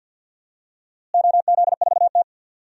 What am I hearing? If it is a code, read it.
Morse code, heard as a keyed tone, about 46 wpm, 700 Hz. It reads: OZ4T
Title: DXCC-100 Call Signs - 4 Characters - Repeated 2x - 50wpm